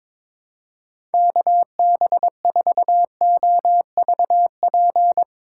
Morse code, heard as a keyed tone, about 22 wpm, 700 Hz. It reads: KB4OVP
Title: USA Call Signs - 2x Character Spacing - 22wpm